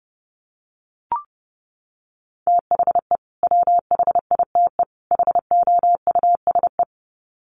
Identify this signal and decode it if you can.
Morse code, heard as a keyed tone, about 30 wpm, 700 Hz. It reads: THE WHITE HOUSE
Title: Sets of 3 Words 30wpm